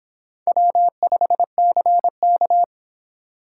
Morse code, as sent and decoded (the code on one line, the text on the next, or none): .-- ..... -.-. -.-
W5CK